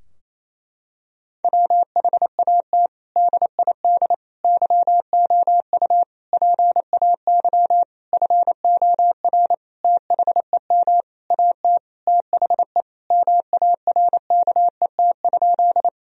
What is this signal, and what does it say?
Morse code, heard as a keyed tone, about 28 wpm, 700 Hz. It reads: WHAT DID YOU PAY FOR THEM AT THE MARKET?
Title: Sentences from Top 1000 Words - Review Part 3 - 28wpm